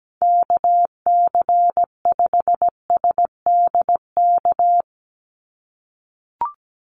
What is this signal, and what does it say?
Morse code, heard as a keyed tone, about 17 wpm, 700 Hz. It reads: KC5SDK
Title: USA Call Signs 17wpm